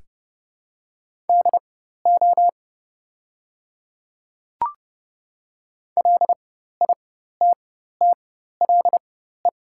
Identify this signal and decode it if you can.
Morse code, heard as a keyed tone, about 30 wpm, spaced as for 14 wpm, 700 Hz. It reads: DO LITTLE
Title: 4x Character Spacing - Top 300 Words - 30wpm